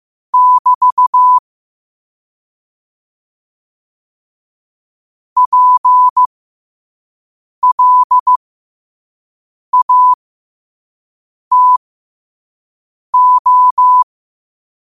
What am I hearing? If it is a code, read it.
Morse code, heard as a keyed tone, about 15 wpm, spaced as for 5 wpm, 1000 Hz. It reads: <BT> PLATO
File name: qotdfiles / 5wpm